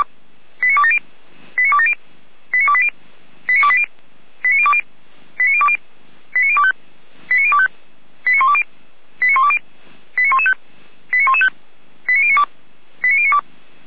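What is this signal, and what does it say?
Five-tone selective call (ZVEI), heard as quick runs of five tones, tones of about 70 ms. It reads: 89270 89270 89280 89280 89020 89020 89025 89025 89120 89120 89115 89115 89092 89092
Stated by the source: ZVEI five-tone selective call received on an SDR.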